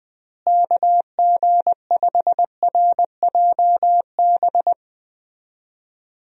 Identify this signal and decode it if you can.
Morse code, heard as a keyed tone, about 20 wpm, 700 Hz. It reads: KG5RJB
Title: USA Call Signs - 6 Characters - Repeated 3x - 20wpm